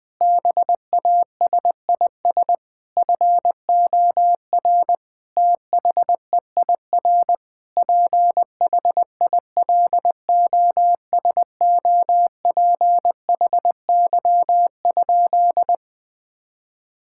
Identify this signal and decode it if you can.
Morse code, heard as a keyed tone, about 20 wpm, 700 Hz. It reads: BASIS FOR THEIR PHILOSOPHY?
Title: Sentences from Top 1000 Words - Review Part 6 - 20wpm